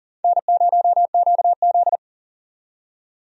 Morse code, heard as a keyed tone, about 40 wpm, 700 Hz. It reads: N0QZ
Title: USA Call Signs - 4 Characters - 40wpm